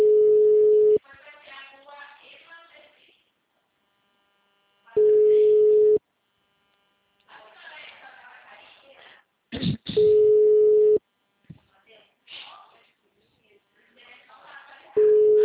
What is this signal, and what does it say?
Ringback tone (Europe, 425 Hz), with speech.
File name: out- -2013- - - .36887